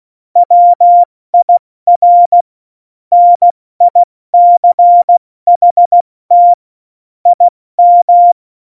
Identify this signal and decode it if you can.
Morse code, heard as a keyed tone, about 16 wpm, 700 Hz. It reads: WIR NICHT IM
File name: Morsekurs / Witze / Witz 06. November 2018, 70 F80